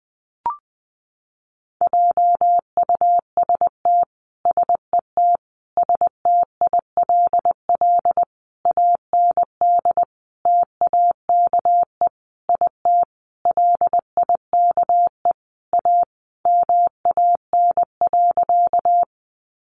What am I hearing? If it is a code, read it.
Morse code, heard as a keyed tone, about 20 wpm, 700 Hz. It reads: JUST SET STILL AND TAKE IT LIKE A MAN.
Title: Sentences from Top 300 Words 20wpm